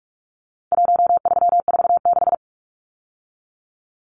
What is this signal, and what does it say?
Morse code, heard as a keyed tone, about 44 wpm, 700 Hz. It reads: 1346
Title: Numbers - Contest Style - 4-Digit 45wpm